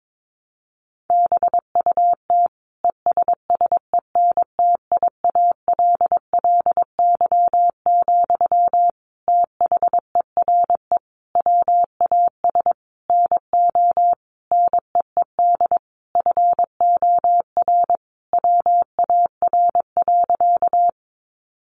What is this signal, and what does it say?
Morse code, heard as a keyed tone, about 22 wpm, 700 Hz. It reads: BUT ESSENTIALLY, THERE WAS NO NEED FOR WAR.
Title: Sentences from Top 1300 Words 22wpm